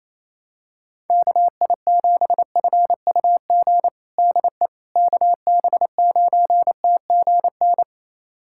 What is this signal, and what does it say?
Morse code, heard as a keyed tone, about 27 wpm, 700 Hz. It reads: KI7FUG DE KB9TGN
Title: USA Callsign DE Callsign - 6 Characters - 28wpm